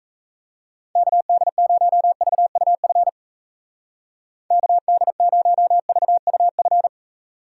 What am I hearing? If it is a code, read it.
Morse code, heard as a keyed tone, about 34 wpm, 700 Hz. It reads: KD0VUF KD0VUF
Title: Speed Racing - USA 2x3 Call Signs 28wpm